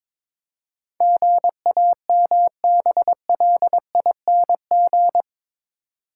Morse code, heard as a keyed tone, about 22 wpm, 700 Hz. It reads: GAMBLING
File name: practice / 22wpm / Eight Letter Words - 22wpm